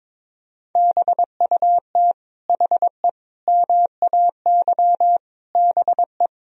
Morse code, heard as a keyed tone, about 22 wpm, 700 Hz. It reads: BUT HE MAY BE
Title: Sentences from Top 1000 Words - Review Part 3 - 22wpm